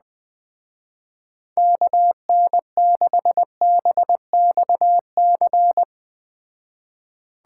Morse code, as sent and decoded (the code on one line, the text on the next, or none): -.- -. -.... -... -..- -.-.
KN6BXC